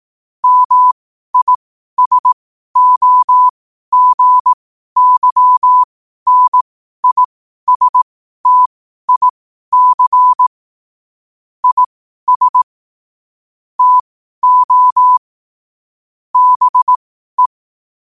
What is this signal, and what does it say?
Morse code, heard as a keyed tone, about 18 wpm, spaced as for 12 wpm, 1000 Hz. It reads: MISOGYNISTIC IS TO BE
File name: qotdfiles / 12wpm